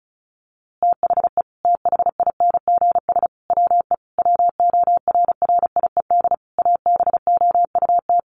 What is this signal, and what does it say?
Morse code, heard as a keyed tone, about 34 wpm, 700 Hz. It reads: THE THINGS WE WORRIED ABOUT
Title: A Sound Of Thunder 35wpm